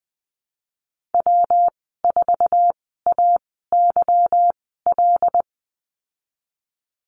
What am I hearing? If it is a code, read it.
Morse code, heard as a keyed tone, about 20 wpm, spaced as for 15 wpm, 700 Hz. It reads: W4AYL
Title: 147 Lesson 24g - Letter y - Call Signs 20wpm